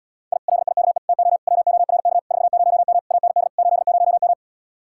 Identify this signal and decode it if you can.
Morse code, heard as a keyed tone, about 77 wpm, 700 Hz. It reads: I LIKE TO WORK 40M AND 80M